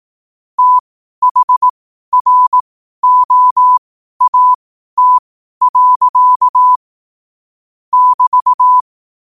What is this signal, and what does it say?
Morse code, heard as a keyed tone, about 18 wpm, spaced as for 12 wpm, 1000 Hz. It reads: THROAT. <BT>
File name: qotdfiles / 12wpm